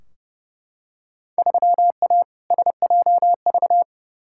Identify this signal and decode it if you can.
Morse code, heard as a keyed tone, about 31 wpm, 700 Hz. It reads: 3A SJV
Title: Field Day Exchanges Encore - 30wpm